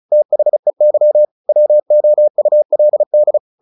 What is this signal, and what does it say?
Morse code, heard as a keyed tone, about 34 wpm, 600 Hz. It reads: THEY WOULD
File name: the old man and the sea / the old man and the sea 1 35wpm